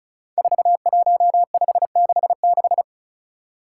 Morse code, heard as a keyed tone, about 36 wpm, 700 Hz. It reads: 41566
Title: Five Challenging Letter-Number Combinations 35wpm